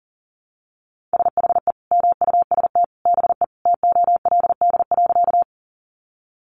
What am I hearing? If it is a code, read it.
Morse code, heard as a keyed tone, about 40 wpm, 700 Hz. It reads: SHE MUST BE TOLD.
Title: Sentences from Top 400 Words 40wpm